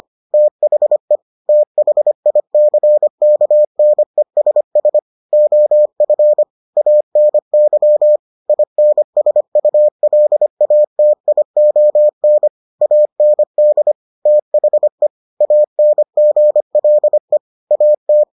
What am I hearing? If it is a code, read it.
Morse code, heard as a keyed tone, about 25 wpm, 600 Hz. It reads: THE THICKNESS OF ANY INSULATION AND THE ANGLE AT